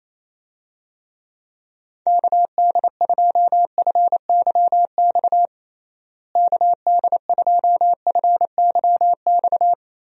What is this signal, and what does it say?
Morse code, heard as a keyed tone, about 28 wpm, 700 Hz. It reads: KD2FYX KD2FYX
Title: USA Call Signs - 6 Characters - Repeated 2x - 28wpm